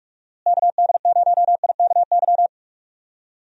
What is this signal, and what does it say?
Morse code, heard as a keyed tone, about 46 wpm, 700 Hz. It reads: KD0IKY
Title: USA Call Signs - 6 Characters - Repeated 3x - 45wpm